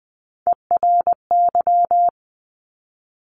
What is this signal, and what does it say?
Morse code, heard as a keyed tone, about 20 wpm, 700 Hz. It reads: ERY
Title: Top Three Letter Combinations in Words 20wpm